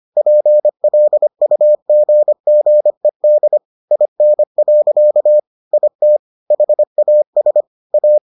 Morse code, heard as a keyed tone, about 25 wpm, 600 Hz. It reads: PLUGGED IN. IT HAS A